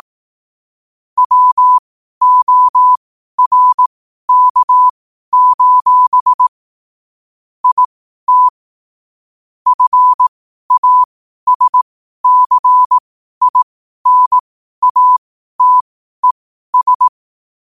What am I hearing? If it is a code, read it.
Morse code, heard as a keyed tone, about 18 wpm, spaced as for 12 wpm, 1000 Hz. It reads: WORK: IT FASCINATES
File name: qotdfiles / 12wpm